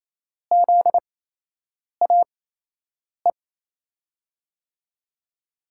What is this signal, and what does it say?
Morse code, heard as a keyed tone, about 28 wpm, spaced as for 8 wpm, 700 Hz. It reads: ZAE
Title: USA Call Signs - 8x Character Spacing - 28wpm